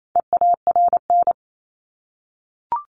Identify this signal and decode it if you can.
Morse code, heard as a keyed tone, about 28 wpm, 700 Hz. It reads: EARN
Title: Top 1251-1500 Words 28wpm